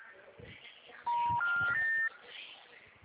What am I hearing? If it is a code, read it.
Special information tone.
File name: out- -2038- - - .40348